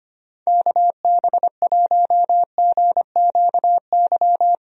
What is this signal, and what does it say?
Morse code, heard as a keyed tone, about 25 wpm, 700 Hz. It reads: KB1GQY
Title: USA Call Signs - 6 Characters - Repeated 3x - 25wpm